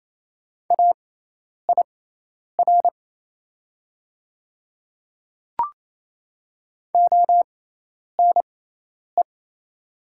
Morse code, heard as a keyed tone, about 28 wpm, spaced as for 10 wpm, 700 Hz. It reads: AIR ONE
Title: 6x Character Spacing - Top 200 Words - 28wpm